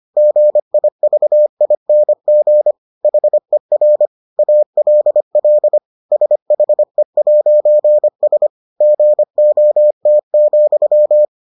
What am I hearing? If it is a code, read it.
Morse code, heard as a keyed tone, about 25 wpm, 600 Hz. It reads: GIVING HER ALL SHE'S GOT,